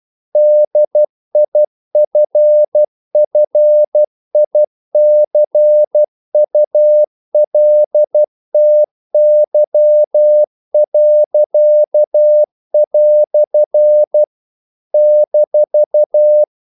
Morse code, heard as a keyed tone, about 12 wpm, 600 Hz. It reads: DIFFICULTY." -
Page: English quotations in CW (telegraphy - morse)